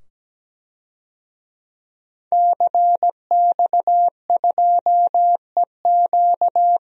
Morse code, heard as a keyed tone, about 17 wpm, 700 Hz. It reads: CX2EQ
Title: DXCC-100 Call Signs - 5 Characters - 17wpm